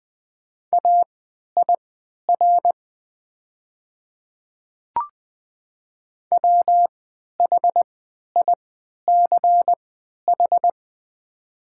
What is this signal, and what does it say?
Morse code, heard as a keyed tone, about 20 wpm, spaced as for 11 wpm, 700 Hz. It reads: AIR WHICH
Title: 3x Character Spacing - Top 200 Words - 20wpm